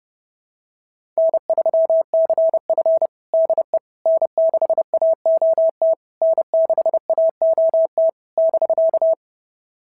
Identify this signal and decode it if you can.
Morse code, heard as a keyed tone, about 30 wpm, 660 Hz. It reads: N3CF DE N6AOT N6AOT <BK>
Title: SOTA QSO Practice 30wpm